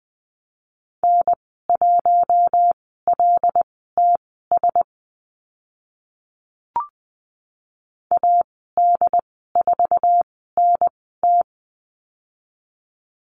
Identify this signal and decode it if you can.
Morse code, heard as a keyed tone, about 20 wpm, spaced as for 15 wpm, 700 Hz. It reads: N1LTS AD4NT
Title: 076 Lesson 13g - Letter L - Call Signs 20wpm